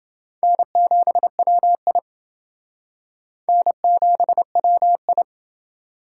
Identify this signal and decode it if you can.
Morse code, heard as a keyed tone, about 29 wpm, 700 Hz. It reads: N7WI N7WI